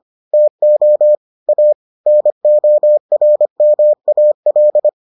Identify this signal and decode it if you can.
Morse code, heard as a keyed tone, about 25 wpm, 600 Hz. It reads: TO A NORMAL